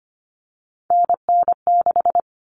Morse code, heard as a keyed tone, about 25 wpm, 700 Hz. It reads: NN6
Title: Common Cut Numbers - 3-Digit 25wpm